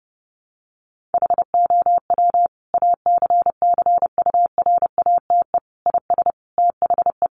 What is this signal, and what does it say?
Morse code, heard as a keyed tone, about 30 wpm, 700 Hz. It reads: HOW ACCURATE IS THE
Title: Sentences from Top 1200 Words 30wpm